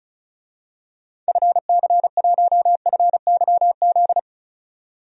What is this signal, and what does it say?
Morse code, heard as a keyed tone, about 36 wpm, 700 Hz. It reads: FC1FYZ
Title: DXCC-100 Call Signs - 6 Characters - 35wpm